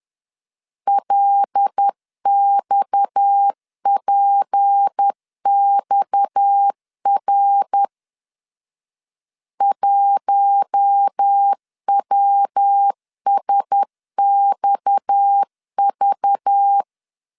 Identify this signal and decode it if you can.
Morse code, heard as a keyed tone, about 10 wpm, 800 Hz. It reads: LXPXR 1WSXV
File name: LEZIONI / 6 PKX0 / 6 PKX 10wpm 2426